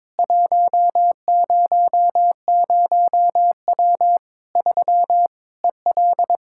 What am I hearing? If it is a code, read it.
Morse code, heard as a keyed tone, about 22 wpm, 700 Hz. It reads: 100W 3 EL
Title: Practice QSO - Exchange 2 - Info - 22wpm